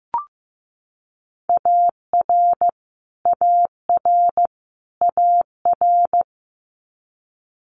Morse code, heard as a keyed tone, about 15 wpm, 700 Hz. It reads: AR AR AR
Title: Multi-letter Phonemes Repeated 3x 15wpm